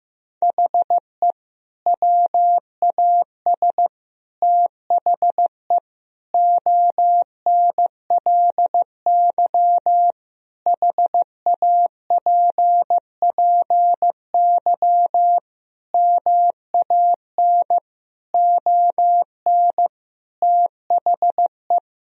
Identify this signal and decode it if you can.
Morse code, heard as a keyed tone, about 15 wpm, 700 Hz. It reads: HE WAS THE ONLY HAPPY MAN ON THE
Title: Sentences from Top 1000 Words - Review Part 3 - 15wpm